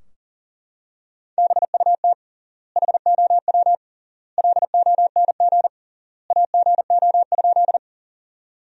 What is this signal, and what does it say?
Morse code, heard as a keyed tone, about 40 wpm, 700 Hz. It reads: BUT HOW LONG AGO?
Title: 3x Word-Spacing - Sentences from Top 500 Words - 40wpm